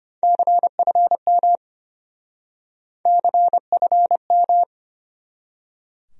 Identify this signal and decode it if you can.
Morse code, heard as a keyed tone, about 33 wpm, 700 Hz. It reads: CFM CFM